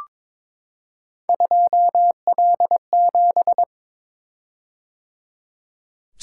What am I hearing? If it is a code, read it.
Morse code, heard as a keyed tone, about 22 wpm, 700 Hz. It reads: 2L7
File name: practice / 22wpm / Three Letter-Number Combinations - 22wpm